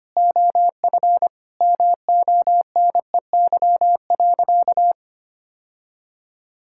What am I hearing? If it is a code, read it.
Morse code, heard as a keyed tone, about 25 wpm, 700 Hz. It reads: OF MONEY.
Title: Sentences from Top 1000 Words - Review Part 6 - 25wpm